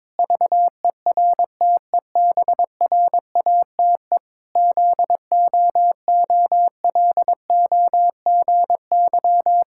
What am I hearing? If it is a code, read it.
Morse code, heard as a keyed tone, about 22 wpm, 700 Hz. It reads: VERTEBRATE ZOOLOGY
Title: Pangram Challenge v2 - 22wpm